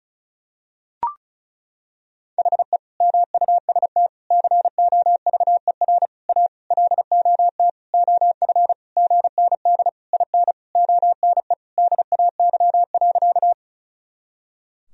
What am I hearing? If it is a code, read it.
Morse code, heard as a keyed tone, about 36 wpm, 700 Hz. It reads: HE MUST COVER A LOT OF GND IN ONE DAY.
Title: Sentences from Top 500 Words and Abbreviations 35wpm